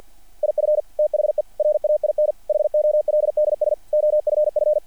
Morse code, heard as a keyed tone, about 48 wpm, 660 Hz. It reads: IF THE RAIN HOLDS OFF
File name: archived / CRSnet